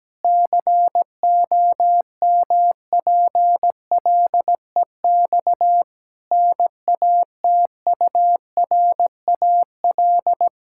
Morse code, heard as a keyed tone, about 17 wpm, 700 Hz. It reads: COMPLEX NATURAL
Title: Sentences from Top 1000 Words - Review Part 6 - 17wpm